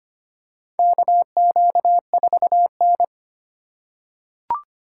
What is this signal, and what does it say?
Morse code, heard as a keyed tone, about 25 wpm, 700 Hz. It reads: KQ4N
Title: USA Call Signs - 4 Characters - 3x Character Spacing - 25wpm